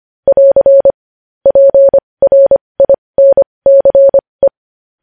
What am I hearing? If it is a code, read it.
Morse code, heard as a keyed tone, about 25 wpm, spaced as for 20 wpm, 550 Hz. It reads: <AR> PRINCE
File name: podcast / News headlines-1x-25x20